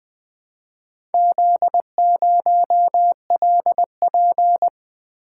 Morse code, heard as a keyed tone, about 20 wpm, 700 Hz. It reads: Z0LP
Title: Four Letter-Number Combinations - 20wpm